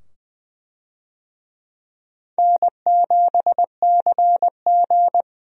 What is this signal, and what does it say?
Morse code, heard as a keyed tone, about 20 wpm, 700 Hz. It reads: N7CG
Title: USA Call Signs - 4 Characters - 20wpm